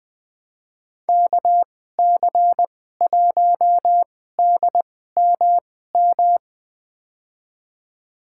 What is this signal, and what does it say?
Morse code, heard as a keyed tone, about 20 wpm, spaced as for 14 wpm, 700 Hz. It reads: KC1DMM
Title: USA Call Signs - 6 Characters - 2x Character Spacing - 20wpm